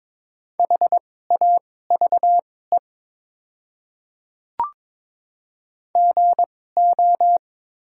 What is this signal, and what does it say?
Morse code, heard as a keyed tone, about 22 wpm, spaced as for 16 wpm, 700 Hz. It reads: HAVE GO
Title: 2x Character Spacing - Top 100 Words - 22wpm